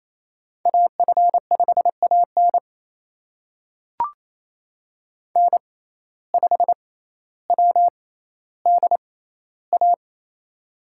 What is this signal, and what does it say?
Morse code, heard as a keyed tone, about 28 wpm, spaced as for 14 wpm, 700 Hz. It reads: AF5AN N5WDA
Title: USA Call Signs - 5 Characters - 6x Character Spacing - 28wpm